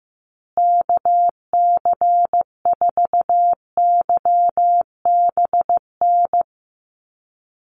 Morse code, heard as a keyed tone, about 15 wpm, 700 Hz. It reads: KC4YBN
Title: USA Call Signs Repeated 3x 15wpm